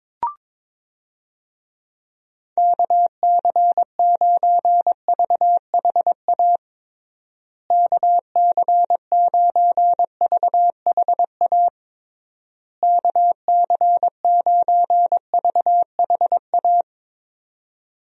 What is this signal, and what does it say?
Morse code, heard as a keyed tone, about 22 wpm, 700 Hz. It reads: KC9VHA KC9VHA KC9VHA
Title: USA Call Signs - 6 Characters - Repeated 3x - 22wpm